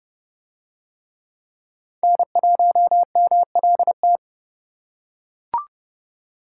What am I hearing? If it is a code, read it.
Morse code, heard as a keyed tone, about 29 wpm, 700 Hz. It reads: N1MLT